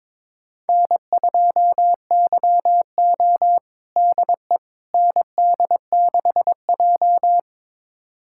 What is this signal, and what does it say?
Morse code, heard as a keyed tone, about 22 wpm, 700 Hz. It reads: N2YO DE ND6J
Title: USA Callsign DE Callsign - 4 Characters - 22wpm